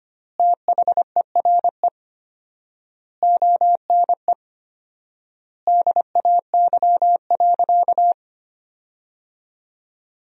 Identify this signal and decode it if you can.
Morse code, heard as a keyed tone, about 25 wpm, 700 Hz. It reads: THERE ONE DAY.
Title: 4x Word-Spacing - Sentences from Top 300 Words - 25wpm